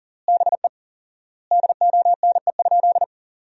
Morse code, heard as a keyed tone, about 40 wpm, 700 Hz. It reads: BE DONE?
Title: 4x Word-Spacing - Sentences from Top 500 Words - 40wpm